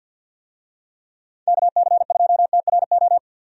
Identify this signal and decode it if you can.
Morse code, heard as a keyed tone, about 52 wpm, 700 Hz. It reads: KC2TRO
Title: USA Call Signs - 6 Characters - 50wpm